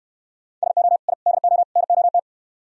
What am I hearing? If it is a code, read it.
Morse code, heard as a keyed tone, about 65 wpm, 700 Hz. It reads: SO I DO NOT